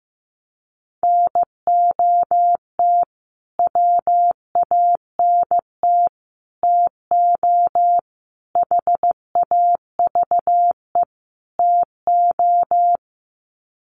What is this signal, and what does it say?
Morse code, heard as a keyed tone, about 15 wpm, 700 Hz. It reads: NOT WANT TO HAVE TO
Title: Sets of 5 Words Encore2 15wpm